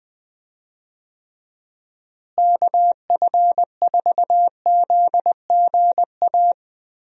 Morse code, heard as a keyed tone, about 21 wpm, 700 Hz. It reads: KF4ZGA
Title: USA Call Signs - 6 Characters - Repeated 3x - 20wpm